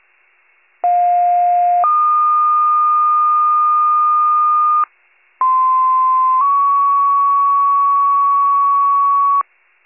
Two-tone page, heard as a one-second tone, then a three-second tone, about 710 then 1220 Hz, about 1040 then 1120 Hz. It is heard from 0.8 to 9.4 s.